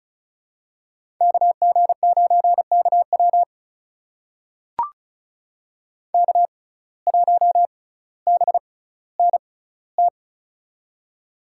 Morse code, heard as a keyed tone, about 36 wpm, spaced as for 18 wpm, 700 Hz. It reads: KG9KW K1BNT
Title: USA Call Signs - 5 Characters - 6x Character Spacing - 35wpm